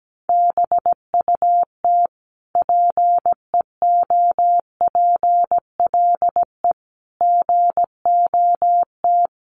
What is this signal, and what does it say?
Morse code, heard as a keyed tone, about 17 wpm, 700 Hz. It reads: BUT PEOPLE GOT
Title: Sentences from Top 700 Words 17wpm